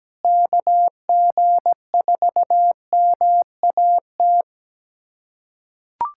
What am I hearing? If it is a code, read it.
Morse code, heard as a keyed tone, about 17 wpm, 700 Hz. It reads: KG4MAT
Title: USA Callsign DE Callsign 17wpm